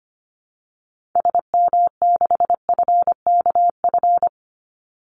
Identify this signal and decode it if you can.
Morse code, heard as a keyed tone, about 25 wpm, 700 Hz. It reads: SM6FKF
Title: Call Signs - 25wpm